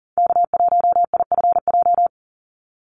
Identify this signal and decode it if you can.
Morse code, heard as a keyed tone, about 41 wpm, 700 Hz. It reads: K1IFJ
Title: Call Signs - 40wpm